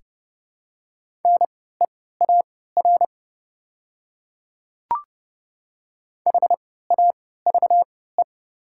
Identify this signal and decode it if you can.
Morse code, heard as a keyed tone, about 30 wpm, spaced as for 17 wpm, 700 Hz. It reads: NEAR HAVE
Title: 3x Character Spacing - Top 200 Words - 30wpm